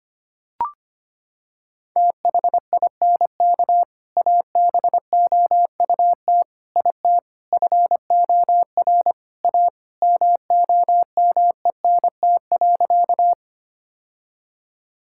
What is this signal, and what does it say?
Morse code, heard as a keyed tone, about 25 wpm, 700 Hz. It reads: THINK ABOUT IT FOR A MOMENT.
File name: practice / 25wpm / Sentences with 3-Word Phrases - 25wpm